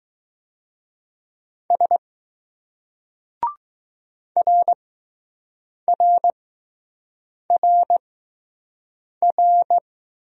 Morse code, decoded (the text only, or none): S R R R R